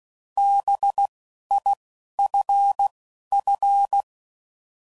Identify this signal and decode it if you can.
Morse code, heard as a keyed tone, about 16 wpm, 800 Hz. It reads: B I F F